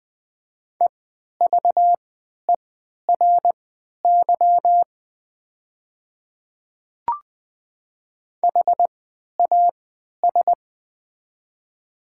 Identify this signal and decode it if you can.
Morse code, heard as a keyed tone, about 20 wpm, spaced as for 11 wpm, 700 Hz. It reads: EVERY HAS
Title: 3x Character Spacing - Top 200 Words - 20wpm